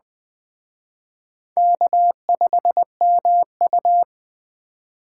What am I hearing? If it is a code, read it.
Morse code, heard as a keyed tone, about 21 wpm, 700 Hz. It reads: K5MU
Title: Speed Racing - USA 1x2 Call Signs 17wpm